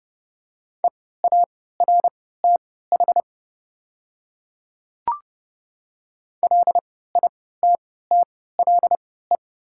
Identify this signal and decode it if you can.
Morse code, heard as a keyed tone, about 29 wpm, spaced as for 17 wpm, 700 Hz. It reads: EARTH LITTLE
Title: 3x Character Spacing - Top 200 Words - 30wpm